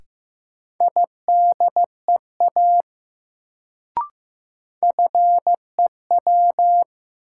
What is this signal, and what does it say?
Morse code, heard as a keyed tone, about 15 wpm, 700 Hz. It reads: IDEA FEW
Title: Sentences from Top 300 Words - New Word Drill - 15wpm